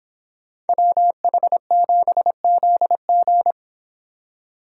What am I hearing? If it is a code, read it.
Morse code, heard as a keyed tone, about 31 wpm, 700 Hz. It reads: WH7ZG